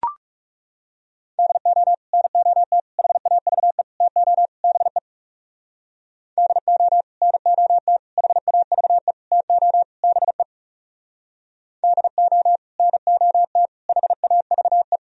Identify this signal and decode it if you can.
Morse code, heard as a keyed tone, about 38 wpm, 700 Hz. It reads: DO NOT HAVE TO BE DO NOT HAVE TO BE DO NOT HAVE